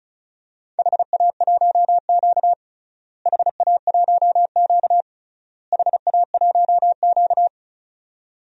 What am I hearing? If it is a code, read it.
Morse code, heard as a keyed tone, about 34 wpm, 700 Hz. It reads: HA1Q HA1Q HA1Q
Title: DXCC-100 Call Signs - 4 Characters - Repeated 3x - 35wpm